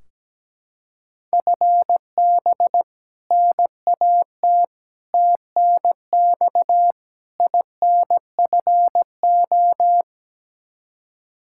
Morse code, heard as a keyed tone, about 17 wpm, 700 Hz. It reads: FB NAT TNX INFO
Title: Practice QSO - Exchange 3 - More Info - 17wpm